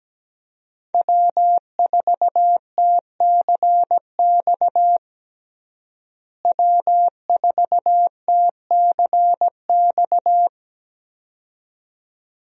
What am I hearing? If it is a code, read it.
Morse code, heard as a keyed tone, about 17 wpm, 700 Hz. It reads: W4TCX W4TCX
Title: USA Call Signs - 5 Characters - Repeated 2x - 17wpm